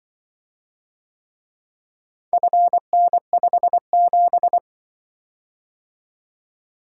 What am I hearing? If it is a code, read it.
Morse code, heard as a keyed tone, about 25 wpm, 700 Hz. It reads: FN57
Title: Speed-Racing - Four Letter-Number Combinations - 20wpm